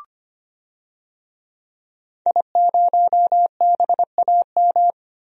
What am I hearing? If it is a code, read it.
Morse code, heard as a keyed tone, about 25 wpm, 700 Hz. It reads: I0BAM
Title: DXCC-100 Call Signs - 5 Characters - Repeated 3x - 25wpm